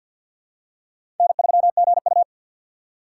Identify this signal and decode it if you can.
Morse code, heard as a keyed tone, about 47 wpm, 700 Hz. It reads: N3GU
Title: USA Call Signs - 4 Characters - 6x Character Spacing - 50wpm